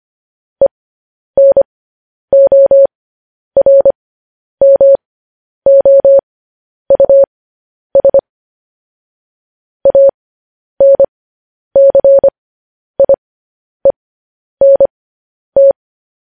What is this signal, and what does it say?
Morse code, heard as a keyed tone, about 25 wpm, spaced as for 10 wpm, 550 Hz. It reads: ENORMOUS ANCIENT
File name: podcast / News headlines-1x-25x10